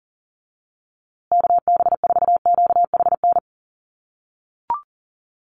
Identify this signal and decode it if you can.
Morse code, heard as a keyed tone, about 40 wpm, 700 Hz. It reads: KB4QHN
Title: USA Call Signs Repeated 3x 40wpm